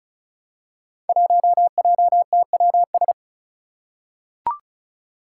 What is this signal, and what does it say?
Morse code, heard as a keyed tone, about 36 wpm, 700 Hz. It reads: 1JTWS
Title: Five Challenging Letter-Number Combinations 35wpm